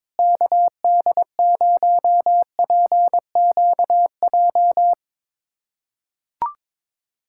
Morse code, heard as a keyed tone, about 22 wpm, 700 Hz. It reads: KD0PQJ
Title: USA Callsign DE Callsign - 6 Characters - 22wpm